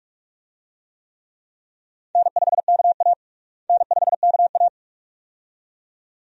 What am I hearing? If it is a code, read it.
Morse code, heard as a keyed tone, about 46 wpm, 700 Hz. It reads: N5KA N5KA
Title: USA Call Signs - 4 Characters - Repeated 2x - 45wpm